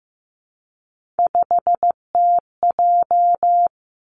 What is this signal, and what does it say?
Morse code, heard as a keyed tone, about 15 wpm, 700 Hz. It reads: 5TJ
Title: Three Challenging Letter-Number Combinations 15wpm